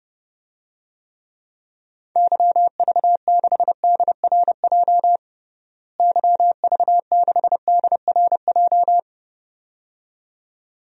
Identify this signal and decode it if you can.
Morse code, heard as a keyed tone, about 30 wpm, 700 Hz. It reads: YV6DRJ YV6DRJ
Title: DXCC-100 Call Signs - 6 Characters - Repeated 2x - 30wpm